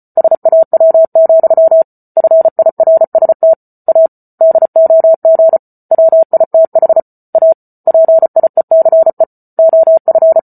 Morse code, heard as a keyed tone, about 34 wpm, 650 Hz. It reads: SAW, FIRST A DOG WITH A PIECE OF